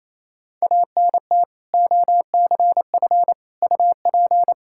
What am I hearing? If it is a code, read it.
Morse code, heard as a keyed tone, about 28 wpm, 700 Hz. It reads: ANT OCF UP
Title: Practice QSO - Exchange 2 - Info - 28wpm